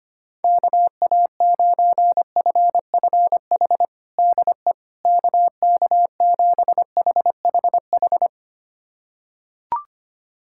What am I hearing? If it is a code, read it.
Morse code, heard as a keyed tone, about 25 wpm, 700 Hz. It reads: KA9FFH DE KK7HHH
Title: USA Callsign DE Callsign - 6 Characters - 25wpm